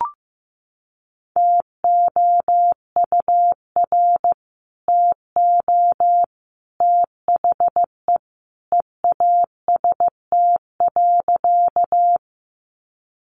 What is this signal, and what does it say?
Morse code, heard as a keyed tone, about 14 wpm, 700 Hz. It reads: TOUR TO THE EAST.
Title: Sentences from Top 1400 Words 15wpm